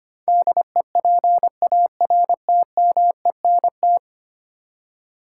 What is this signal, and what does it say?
Morse code, heard as a keyed tone, about 25 wpm, 700 Hz. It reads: DEPARTMENT
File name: Ten Letter Words - 25wpm